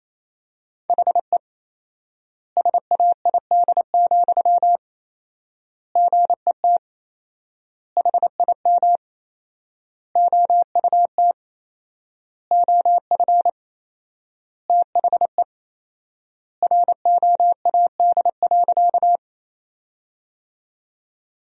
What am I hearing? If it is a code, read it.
Morse code, heard as a keyed tone, about 28 wpm, 700 Hz. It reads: HE SAID, GET HIM OUT OF THE ROAD.
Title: 4x Word-Spacing - Sentences from Top 400 Words - 28wpm